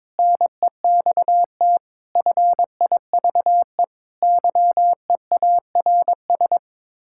Morse code, heard as a keyed tone, about 22 wpm, 700 Hz. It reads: NEXT FIVE YEARS
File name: practice / 22wpm / Build A Phrase - 5-Word Phrases - 22wpm